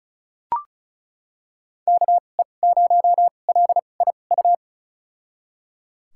Morse code, heard as a keyed tone, about 36 wpm, spaced as for 25 wpm, 700 Hz. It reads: KE0LIU
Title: USA Call Signs - 6 Characters - 2x Character Spacing - 35wpm